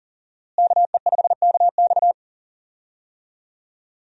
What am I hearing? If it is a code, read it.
Morse code, heard as a keyed tone, about 40 wpm, 700 Hz. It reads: KE5KX
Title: USA Call Signs - 5 Characters - Repeated 3x - 40wpm